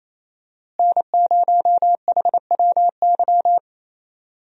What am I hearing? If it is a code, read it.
Morse code, heard as a keyed tone, about 27 wpm, 700 Hz. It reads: N0HWY
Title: USA Call Signs - 5 Characters - Repeated 3x - 28wpm